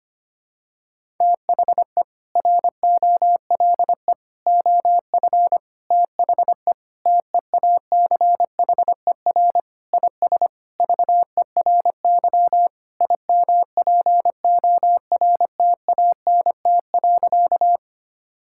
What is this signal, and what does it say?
Morse code, heard as a keyed tone, about 25 wpm, 700 Hz. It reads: THE ROLE OF THE TEACHER IS VERY IMPORTANT.
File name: practice / 25wpm / Sentences with 4-Word Phrases - 25wpm